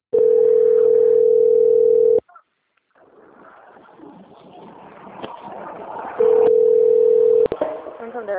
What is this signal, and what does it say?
Ringback tone (North America), 440 + 480 Hz, with speech.
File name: out- - - - - .39625